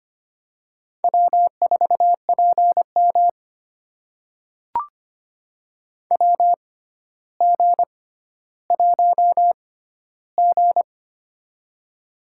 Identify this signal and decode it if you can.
Morse code, heard as a keyed tone, about 25 wpm, spaced as for 13 wpm, 700 Hz. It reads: W4PM WG1G
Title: USA Call Signs - 4 Characters - 6x Character Spacing - 25wpm